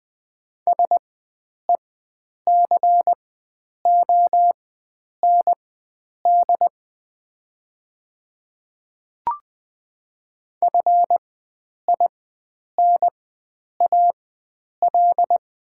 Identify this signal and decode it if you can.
Morse code, heard as a keyed tone, about 20 wpm, spaced as for 9 wpm, 700 Hz. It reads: SECOND FINAL
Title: 4x Character Spacing - Top 500 Words - 20wpm